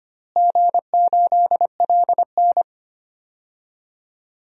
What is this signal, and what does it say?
Morse code, heard as a keyed tone, about 25 wpm, 700 Hz. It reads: G8LN
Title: DXCC-100 Call Signs - 4 Characters - Repeated 2x - 25wpm